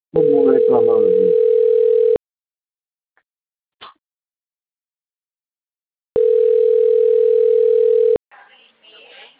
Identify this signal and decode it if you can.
Ringback tone (North America), 440 + 480 Hz, with speech.
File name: rg-1600-unknown- - - .758